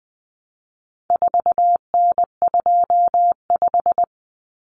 Morse code, heard as a keyed tone, about 20 wpm, 700 Hz. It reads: 4N25
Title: Common Cut Numbers - 4-Digit 20wpm